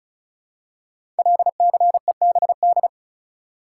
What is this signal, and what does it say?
Morse code, heard as a keyed tone, about 36 wpm, 700 Hz. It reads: LCEBD
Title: Code Groups - 5 Letters - 35wpm